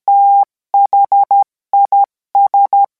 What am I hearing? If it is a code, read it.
Morse code, heard as a keyed tone, about 13 wpm, 800 Hz. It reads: THIS